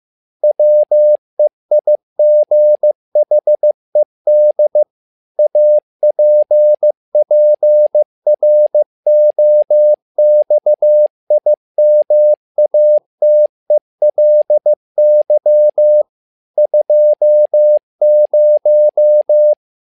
Morse code, heard as a keyed tone, about 15 wpm, 600 Hz. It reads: WEIGHED APPROXIMATELY 20
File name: learning center / tower trouble / tower trouble 15wpm